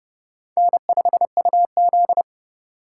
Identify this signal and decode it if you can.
Morse code, heard as a keyed tone, about 29 wpm, 700 Hz. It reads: N5UZ